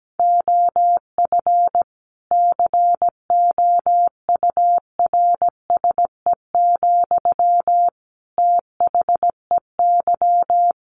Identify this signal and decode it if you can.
Morse code, heard as a keyed tone, about 17 wpm, 700 Hz. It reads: OF COURSE, THEY
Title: Sentences from Top 1400 Words 17wpm